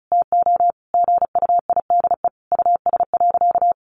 Morse code, heard as a keyed tone, about 34 wpm, 700 Hz. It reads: TO GUIDE US.
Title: Sentences from Top 800 Words 35wpm